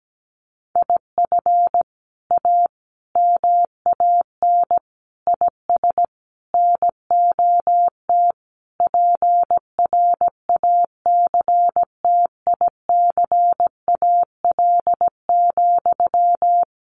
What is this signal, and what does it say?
Morse code, heard as a keyed tone, about 17 wpm, 700 Hz. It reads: IF A MAN IS NOT PRACTICAL,
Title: Sentences from Top 1400 Words 17wpm